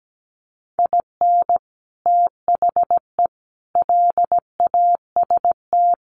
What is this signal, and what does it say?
Morse code, heard as a keyed tone, about 17 wpm, 700 Hz. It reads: IN THE LAST
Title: Sets of 5 Words 17wpm